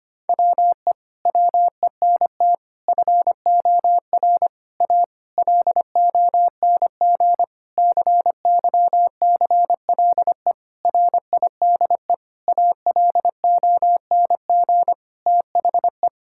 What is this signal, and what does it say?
Morse code, heard as a keyed tone, about 25 wpm, 700 Hz. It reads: WE WENT FOR A LONG CYCLE RIDE ALONG THE
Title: Sentences from Top 1000 Words - Review Part 6 - 25wpm